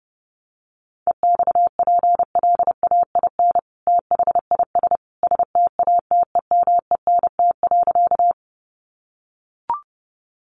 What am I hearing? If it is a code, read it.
Morse code, heard as a keyed tone, about 30 wpm, 700 Hz. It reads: EXPLAIN THIS STATEMENT.
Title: Sentences from Top 1200 Words 30wpm